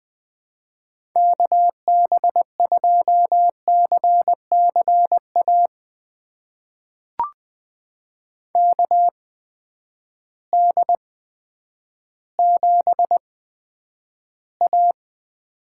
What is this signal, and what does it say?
Morse code, heard as a keyed tone, about 21 wpm, spaced as for 10 wpm, 700 Hz. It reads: KB2CCA KD7A
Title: USA Call Signs - 8x Character Spacing - 20wpm